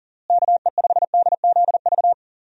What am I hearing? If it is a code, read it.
Morse code, heard as a keyed tone, about 40 wpm, 700 Hz. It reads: KE5DZV
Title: USA Call Signs - 6 Characters - Repeated 2x - 40wpm